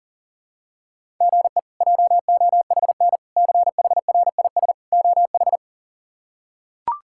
Morse code, heard as a keyed tone, about 40 wpm, 700 Hz. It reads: GE JOHN CHRIS OH
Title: K1USN SST QSO Practice - 40wpm